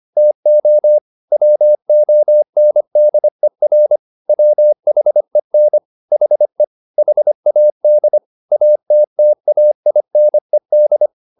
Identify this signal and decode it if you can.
Morse code, heard as a keyed tone, about 25 wpm, 600 Hz. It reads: TO WONDER WHEN HE HAD ATTAINED
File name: the old man and the sea part 1 25wpm